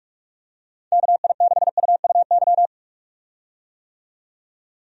Morse code, heard as a keyed tone, about 47 wpm, 700 Hz. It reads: KI6UUY
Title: USA Call Signs - 6 Characters - 45wpm